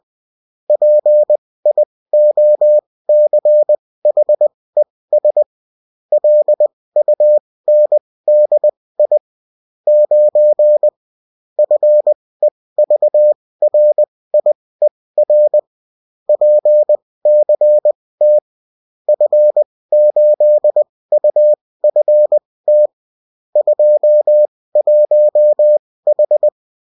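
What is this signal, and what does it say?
Morse code, heard as a keyed tone, about 20 wpm, spaced as for 16 wpm, 600 Hz. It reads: PIOCHES LUNDI 9 FEVRIER PCT F8UFT 21H
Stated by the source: Morse code bulletin of a French amateur CW union, sent as text and in CW.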